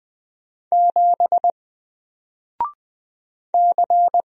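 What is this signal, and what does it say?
Morse code, heard as a keyed tone, about 20 wpm, 700 Hz. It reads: 7 C
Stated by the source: Single letter and number Morse drill at 20 wpm, from N4LCD.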